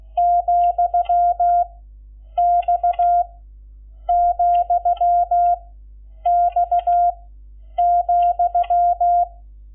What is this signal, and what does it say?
Morse code, heard as a keyed tone, about 16 wpm, 700 Hz. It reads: , X , X ,
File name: Lecciones / CLASE Nº11